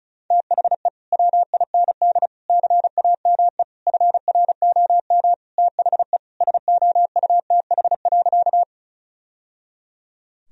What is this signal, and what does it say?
Morse code, heard as a keyed tone, about 36 wpm, 700 Hz. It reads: THE WIND CAME FROM THE SOUTH.
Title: Sentences from Top 1000 Words - Review Part 2 - 35wpm